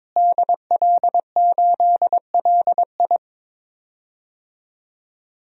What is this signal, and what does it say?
Morse code, heard as a keyed tone, about 22 wpm, 700 Hz. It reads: DL8LI
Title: DXCC-100 Call Signs - 5 Characters - Repeated 3x - 22wpm